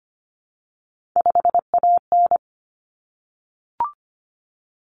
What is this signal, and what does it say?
Morse code, heard as a keyed tone, about 25 wpm, 700 Hz. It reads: 5AN
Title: Common Cut Numbers - 3-Digit 25wpm